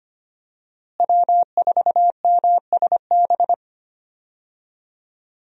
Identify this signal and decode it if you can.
Morse code, heard as a keyed tone, about 25 wpm, 700 Hz. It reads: W4MSB
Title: USA Call Signs - 5 Characters - Repeated 2x - 25wpm